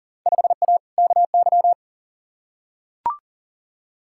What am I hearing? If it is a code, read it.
Morse code, heard as a keyed tone, about 40 wpm, 700 Hz. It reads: 5A KY
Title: Field Day Exchanges - 40wpm